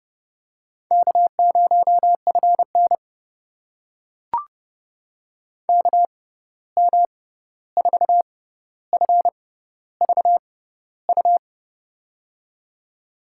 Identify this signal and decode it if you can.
Morse code, heard as a keyed tone, about 30 wpm, spaced as for 14 wpm, 700 Hz. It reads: K0FN KM4FVU
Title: USA Call Signs - 6x Character Spacing - 30wpm